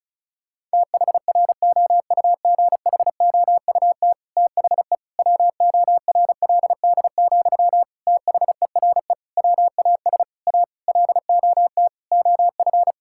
Morse code, heard as a keyed tone, about 36 wpm, 700 Hz. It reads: THROUGHOUT THE WORLD, THERE WAS A LOT OF
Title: Sentences from Top 1000 Words - Review Part 5 - 35wpm